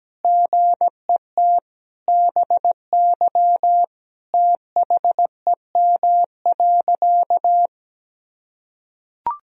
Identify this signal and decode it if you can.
Morse code, heard as a keyed tone, about 17 wpm, 700 Hz. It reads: GET BY THEM.
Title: Sentences from Top 1000 Words - Review Part 1 - 17wpm